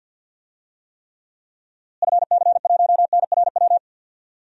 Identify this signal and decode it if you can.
Morse code, heard as a keyed tone, about 50 wpm, 700 Hz. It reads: FC1NRW
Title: DXCC-100 Call Signs - 6 Characters - Repeated 3x - 50wpm